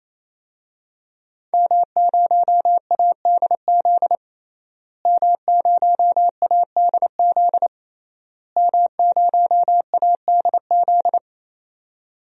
Morse code, heard as a keyed tone, about 27 wpm, 700 Hz. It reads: M0ADZ M0ADZ M0ADZ
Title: DXCC-100 Call Signs - 5 Characters - Repeated 3x - 28wpm